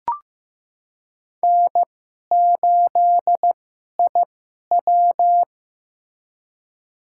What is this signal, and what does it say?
Morse code, heard as a keyed tone, about 15 wpm, spaced as for 11 wpm, 700 Hz. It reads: N8IW
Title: USA Call Signs - 4 Characters - 2x Character Spacing - 15wpm